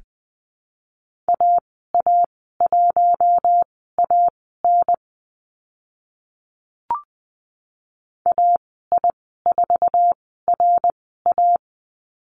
Morse code, heard as a keyed tone, about 20 wpm, spaced as for 15 wpm, 700 Hz. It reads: AA1AN AI4RA
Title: 055 Lesson 10g - Letter r - Call Signs 20wpm